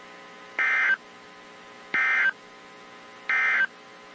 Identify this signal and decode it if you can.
Emergency Alert System (SAME) tones, heard as harsh buzzing bursts. It is heard from 0.5 to 3.7 s.
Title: WZ2523 - NOAA Weather Radio